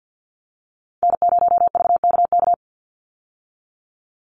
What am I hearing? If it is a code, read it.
Morse code, heard as a keyed tone, about 50 wpm, 700 Hz. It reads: N0VKK
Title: USA Call Signs 50wpm